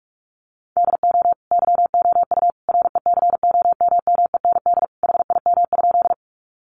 Morse code, heard as a keyed tone, about 44 wpm, 700 Hz. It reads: DO YOU RECOMMEND HIM?
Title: Sentences from Top 1200 Words 45wpm